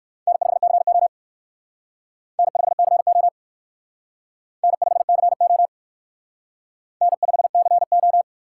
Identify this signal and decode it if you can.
Morse code, heard as a keyed tone, about 56 wpm, 700 Hz. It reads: N5CO N5CO N5CO N5CO